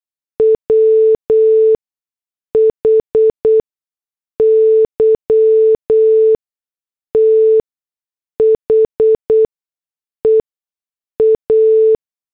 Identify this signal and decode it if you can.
Morse code, heard as a keyed tone, about 8 wpm, spaced as for 6 wpm, 440 Hz. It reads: WHY THE A